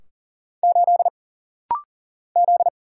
Morse code, heard as a keyed tone, about 40 wpm, 700 Hz. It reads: 8 Z
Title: Single Letter-Number - Rapid-Fire 40wpm